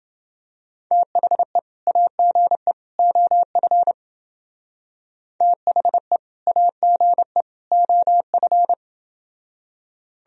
Morse code, heard as a keyed tone, about 30 wpm, 700 Hz. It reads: THE AGE OF THE AGE OF